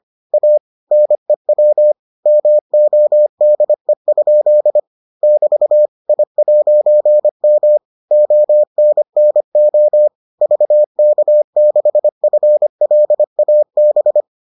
Morse code, heard as a keyed tone, about 25 wpm, 600 Hz. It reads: A NEW MODE? <BT> I'M ONNO VK6FLAB